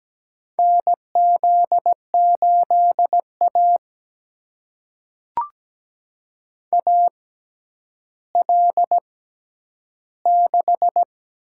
Morse code, heard as a keyed tone, about 17 wpm, spaced as for 10 wpm, 700 Hz. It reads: NZ8A AL6
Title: USA Call Signs - 4 Characters - 6x Character Spacing - 17wpm